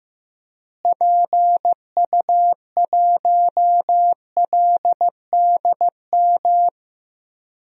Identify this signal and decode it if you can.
Morse code, heard as a keyed tone, about 15 wpm, 700 Hz. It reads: PU1LDM
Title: DXCC-100 Call Signs - 6 Characters - Repeated 3x - 15wpm